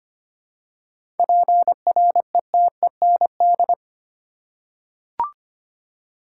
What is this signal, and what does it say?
Morse code, heard as a keyed tone, about 25 wpm, 700 Hz. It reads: PRETEND
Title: Seven Letter Words - 25wpm